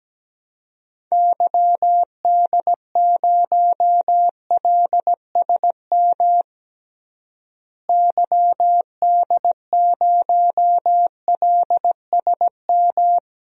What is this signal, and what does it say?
Morse code, heard as a keyed tone, about 17 wpm, 700 Hz. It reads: YD0LSM YD0LSM
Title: DXCC-100 Call Signs - 6 Characters - Repeated 3x - 17wpm